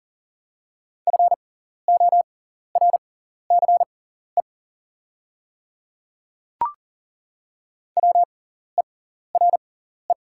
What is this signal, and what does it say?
Morse code, heard as a keyed tone, about 37 wpm, spaced as for 14 wpm, 700 Hz. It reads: FORCE WERE
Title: 6x Character Spacing - Top 500 Words - 40wpm